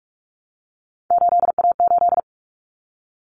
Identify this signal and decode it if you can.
Morse code, heard as a keyed tone, about 44 wpm, 700 Hz. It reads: 8A8
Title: Common Cut Numbers - 3-Digit 45wpm